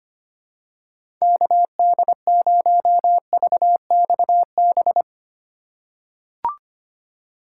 Morse code, heard as a keyed tone, about 25 wpm, 700 Hz. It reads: KD0VXB
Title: USA Call Signs - 6 Characters - 2x Character Spacing - 25wpm